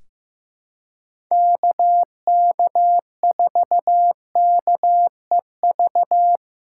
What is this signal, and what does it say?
Morse code, heard as a keyed tone, about 15 wpm, 700 Hz. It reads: KK4KEV
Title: USA Callsign DE Callsign - 6 Characters - 15wpm